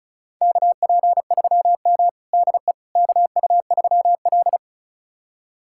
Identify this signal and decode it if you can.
Morse code, heard as a keyed tone, about 34 wpm, 700 Hz. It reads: KP3M DE KU3L
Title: USA Callsign DE Callsign - 4 Characters - 35wpm